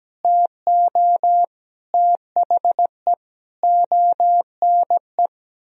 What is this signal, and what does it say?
Morse code, heard as a keyed tone, about 17 wpm, 700 Hz. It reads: TO THE ONE
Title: Sentences from Top 1000 Words - Review Part 2 - 17wpm